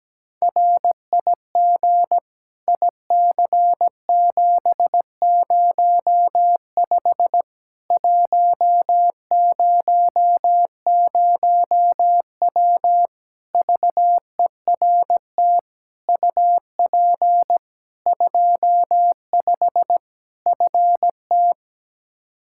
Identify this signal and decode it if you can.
Morse code, heard as a keyed tone, about 17 wpm, 700 Hz. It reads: RIG IC705 100W VERT UP 25 FT
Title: K1USN SST QSO Practice 17wpm